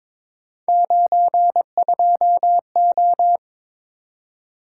Morse code, heard as a keyed tone, about 22 wpm, 700 Hz. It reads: 92O
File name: practice / 22wpm / Three Letter-Number Combinations - 22wpm